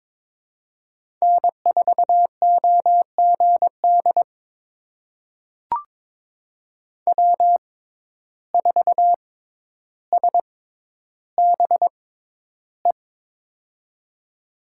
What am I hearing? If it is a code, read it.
Morse code, heard as a keyed tone, about 22 wpm, spaced as for 11 wpm, 700 Hz. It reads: N4OGD W4SBE
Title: USA Call Signs - 6x Character Spacing - 22wpm